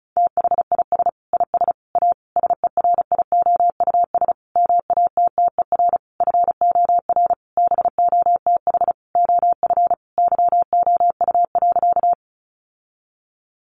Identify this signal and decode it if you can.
Morse code, heard as a keyed tone, about 34 wpm, 700 Hz. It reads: THIS IS A SERIOUS MATTER FOR BOTH OF YOU.
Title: Sentences from Top 1100 Words 35wpm